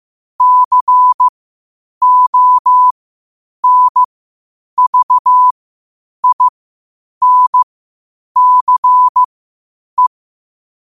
Morse code, heard as a keyed tone, about 15 wpm, spaced as for 8 wpm, 1000 Hz. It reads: CONVINCE
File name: qotdfiles / 8wpm